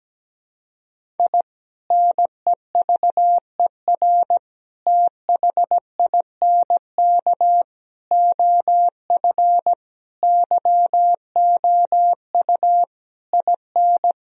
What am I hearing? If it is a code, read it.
Morse code, heard as a keyed tone, about 17 wpm, 700 Hz. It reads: I NEVER THINK OF YOU IN
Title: Sentences from Top 1000 Words - Review Part 1 - 17wpm